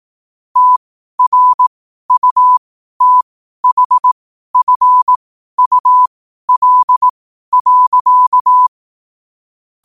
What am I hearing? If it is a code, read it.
Morse code, heard as a keyed tone, about 18 wpm, spaced as for 12 wpm, 1000 Hz. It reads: TRUTHFUL.
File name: qotdfiles / 12wpm